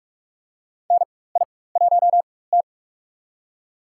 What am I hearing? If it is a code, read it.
Morse code, heard as a keyed tone, about 47 wpm, spaced as for 21 wpm, 700 Hz. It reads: NI1T
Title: USA Call Signs - 4x Character Spacing - 45wpm